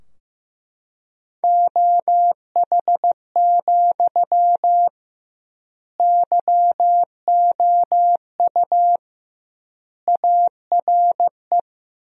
Morse code, heard as a keyed tone, about 16 wpm, 700 Hz. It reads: OH, YOU ARE
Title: 2x Word-Spacing - Sentences from Top 500 Words - 15wpm